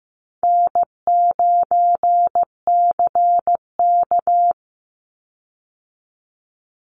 Morse code, heard as a keyed tone, about 15 wpm, 700 Hz. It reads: N9CK
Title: Easy Call Signs Repeated 3x - 15wpm